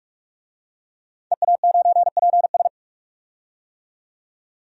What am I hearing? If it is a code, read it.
Morse code, heard as a keyed tone, about 46 wpm, 700 Hz. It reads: EA9PS
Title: DXCC-100 Call Signs - 5 Characters - 45wpm